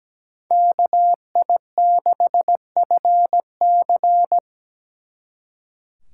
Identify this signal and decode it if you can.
Morse code, heard as a keyed tone, about 17 wpm, 700 Hz. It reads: KI6FC
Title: Speed Racing - USA 2x2 Call Signs 17wpm